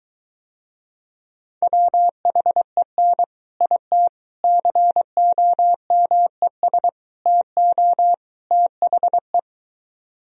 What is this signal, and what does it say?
Morse code, heard as a keyed tone, about 22 wpm, 700 Hz. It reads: WHEN IT COMES TO THE